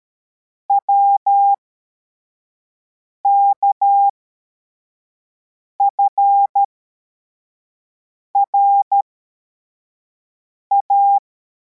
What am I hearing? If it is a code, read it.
Morse code, heard as a keyed tone, about 13 wpm, spaced as for 4 wpm, 800 Hz. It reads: WKFRA